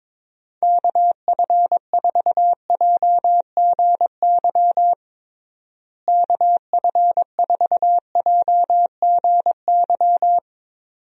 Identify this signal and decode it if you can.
Morse code, heard as a keyed tone, about 22 wpm, 700 Hz. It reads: KF4JGY KF4JGY
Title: USA Call Signs - 6 Characters - Repeated 2x - 22wpm